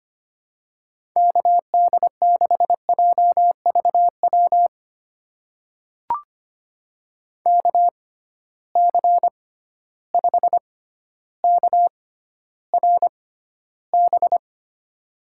Morse code, heard as a keyed tone, about 25 wpm, spaced as for 13 wpm, 700 Hz. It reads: KD6JVW KC5KRB
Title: USA Call Signs - 6 Characters - 6x Character Spacing - 25wpm